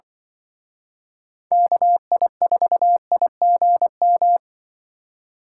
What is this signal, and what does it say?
Morse code, heard as a keyed tone, about 25 wpm, 700 Hz. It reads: KI4IGM